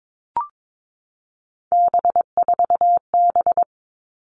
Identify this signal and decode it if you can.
Morse code, heard as a keyed tone, about 22 wpm, 700 Hz. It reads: B4B
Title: All Cut Numbers - 3-Digit 22wpm